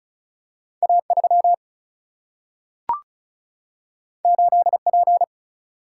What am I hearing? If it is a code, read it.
Morse code, heard as a keyed tone, about 36 wpm, 700 Hz. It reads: A3 8P
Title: Single-Letter Single-Number 35wpm